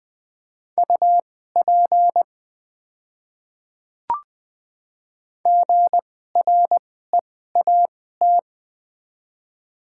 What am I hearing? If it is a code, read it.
Morse code, heard as a keyed tone, about 20 wpm, spaced as for 14 wpm, 700 Hz. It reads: UP GREAT
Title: 2x Character Spacing - Top 200 Words - 20wpm